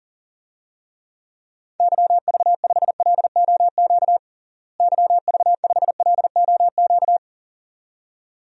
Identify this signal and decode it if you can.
Morse code, heard as a keyed tone, about 40 wpm, 700 Hz. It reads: YV5LOQ YV5LOQ
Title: DXCC-100 Call Signs - 6 Characters - Repeated 2x - 40wpm